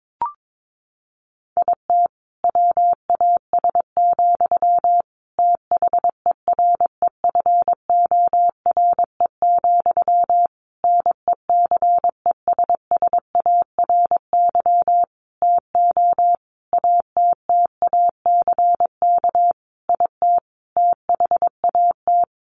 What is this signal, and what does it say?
Morse code, heard as a keyed tone, about 22 wpm, 700 Hz. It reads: IT WAS, THEREFORE, NECESSARY TO ATTACK IT THAT
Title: Sentences from Top 1500 Words 22wpm